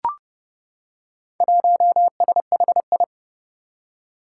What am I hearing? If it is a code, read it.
Morse code, heard as a keyed tone, about 30 wpm, 700 Hz. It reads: 1SHI
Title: Four Challenging Letter-Number Combinations 30wpm